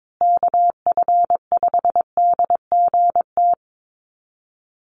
Morse code, heard as a keyed tone, about 22 wpm, 700 Hz. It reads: KF5DGT
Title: USA Call Signs Repeated 3x 22wpm